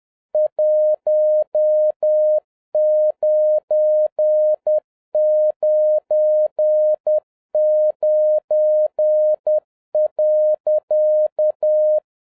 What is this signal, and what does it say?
Morse code, heard as a keyed tone, about 10 wpm, 610 Hz. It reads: 1999.